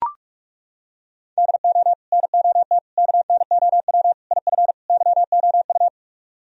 Morse code, heard as a keyed tone, about 40 wpm, 700 Hz. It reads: DO NOT KNOW IF YOU